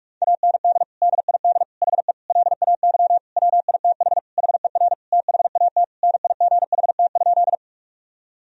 Morse code, heard as a keyed tone, about 43 wpm, 700 Hz. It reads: AND DID HE LAY WITH HER THAT NIGHT?
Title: Sentences from Top 1000 Words - Review Part 2 - 45wpm